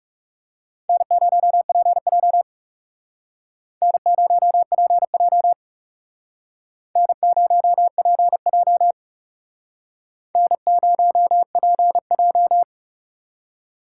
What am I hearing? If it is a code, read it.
Morse code, heard as a keyed tone, about 38 wpm, 700 Hz. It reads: N0PJ N0PJ N0PJ N0PJ